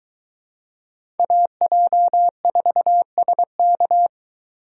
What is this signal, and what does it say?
Morse code, heard as a keyed tone, about 23 wpm, 700 Hz. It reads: AJ4SK